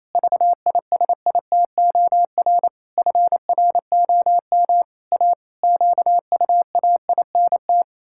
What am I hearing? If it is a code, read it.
Morse code, heard as a keyed tone, about 28 wpm, 700 Hz. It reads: VISITOR FROM A QUAINT
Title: Pangram Challenge v2 - 28wpm